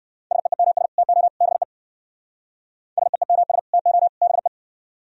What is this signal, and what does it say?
Morse code, heard as a keyed tone, about 59 wpm, 700 Hz. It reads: SEEMS TO BE SEEMS TO BE